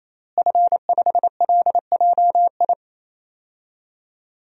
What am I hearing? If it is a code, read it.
Morse code, heard as a keyed tone, about 28 wpm, 700 Hz. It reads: F5LJI
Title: DXCC-100 Call Signs - 5 Characters - Repeated 3x - 28wpm